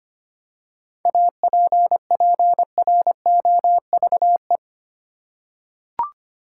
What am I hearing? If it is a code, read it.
Morse code, heard as a keyed tone, about 25 wpm, 700 Hz. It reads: APPROVE
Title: Seven Letter Words - 25wpm